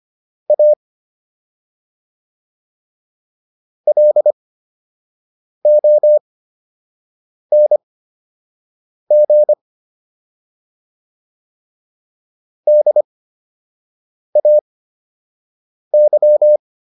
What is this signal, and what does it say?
Morse code, heard as a keyed tone, about 25 wpm, spaced as for 6 wpm, 600 Hz. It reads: A LONG DAY